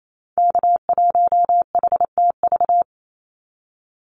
Morse code, heard as a keyed tone, about 28 wpm, 700 Hz. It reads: K1HTV
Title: Call Signs - 28wpm